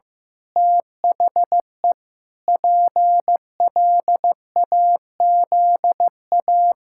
Morse code, heard as a keyed tone, about 15 wpm, 700 Hz. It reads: THE PLAZA
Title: Pangram Challenge v2 - 15wpm